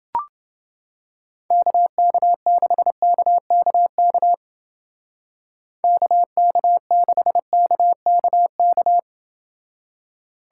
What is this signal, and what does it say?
Morse code, heard as a keyed tone, about 29 wpm, 700 Hz. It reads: KK6KKK KK6KKK